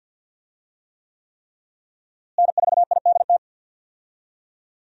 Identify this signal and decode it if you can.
Morse code, heard as a keyed tone, about 46 wpm, 700 Hz. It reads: N4IDT
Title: USA Call Signs - 5 Characters - Repeated 3x - 50wpm